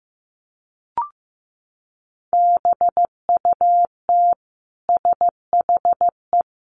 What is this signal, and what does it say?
Morse code, heard as a keyed tone, about 15 wpm, 700 Hz. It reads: BUT SHE
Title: Sentences from Top 1400 Words 15wpm